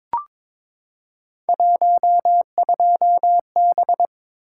Morse code, heard as a keyed tone, about 22 wpm, 700 Hz. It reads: 12B
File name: practice / 22wpm / Three Letter-Number Combinations - 22wpm